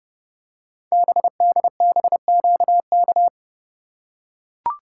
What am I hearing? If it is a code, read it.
Morse code, heard as a keyed tone, about 30 wpm, 700 Hz. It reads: BDBQK
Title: Code Groups - 5 Letters - 30wpm